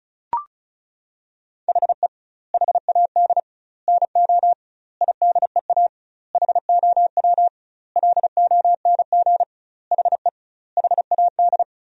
Morse code, heard as a keyed tone, about 34 wpm, 700 Hz. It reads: HE HAD NO IDEA HOW LONG HE HAD
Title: 2x Word-Spacing - Sentences from Top 300 Words - 35wpm